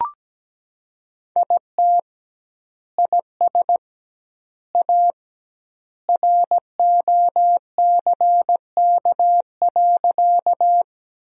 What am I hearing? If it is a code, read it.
Morse code, heard as a keyed tone, about 17 wpm, 700 Hz. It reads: IT IS A ROCK.
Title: 2x Word-Spacing - Sentences from Top 400 Words - 17wpm